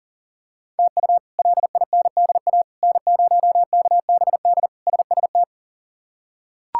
Morse code, heard as a keyed tone, about 40 wpm, 700 Hz. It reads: TU LINDA N0KBD SST
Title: K1USN SST QSO Practice - 40wpm